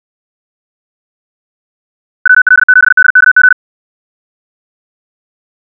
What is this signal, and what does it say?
Morse code, heard as a keyed tone, about 64 wpm, 1500 Hz. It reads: WL7DKG
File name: ja7uhv / SampleHST3